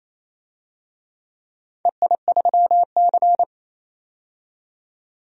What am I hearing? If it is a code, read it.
Morse code, heard as a keyed tone, about 28 wpm, 700 Hz. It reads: EI3C
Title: DXCC-100 Call Signs - 4 Characters - 28wpm